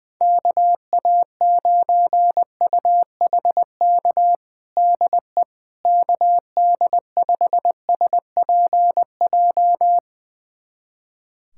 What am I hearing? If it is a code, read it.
Morse code, heard as a keyed tone, about 20 wpm, 700 Hz. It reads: KA9UHK DE KD5SPJ
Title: USA Callsign DE Callsign - 6 Characters - 20wpm